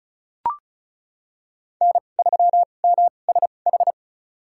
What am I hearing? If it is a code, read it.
Morse code, heard as a keyed tone, about 36 wpm, spaced as for 25 wpm, 700 Hz. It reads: N3MSH
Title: USA Call Signs - 5 Characters - 2x Character Spacing - 35wpm